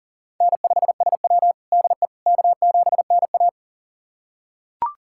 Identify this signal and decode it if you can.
Morse code, heard as a keyed tone, about 40 wpm, 700 Hz. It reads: N5SW DE K7NA
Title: USA Callsign DE Callsign - 4 Characters - 40wpm